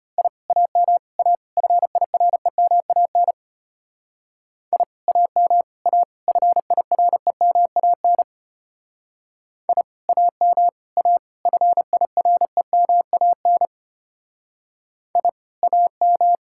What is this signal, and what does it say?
Morse code, heard as a keyed tone, about 32 wpm, 700 Hz. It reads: I AM A FIREMAN I AM A FIREMAN I AM A FIREMAN I AM